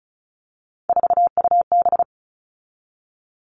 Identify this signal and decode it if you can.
Morse code, heard as a keyed tone, about 34 wpm, 700 Hz. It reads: 4UB
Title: All Cut Numbers - 3-Digit 35wpm